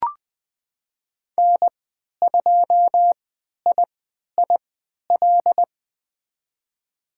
Morse code, heard as a keyed tone, about 20 wpm, spaced as for 11 wpm, 700 Hz. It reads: N2IIL
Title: USA Call Signs - 3x Character Spacing - 20wpm